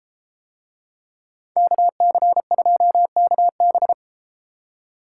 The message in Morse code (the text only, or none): KC2KB